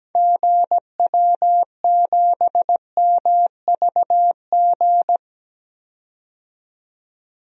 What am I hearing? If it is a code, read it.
Morse code, heard as a keyed tone, about 17 wpm, 700 Hz. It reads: GW7MVG
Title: DXCC-100 Call Signs - 6 Characters - 17wpm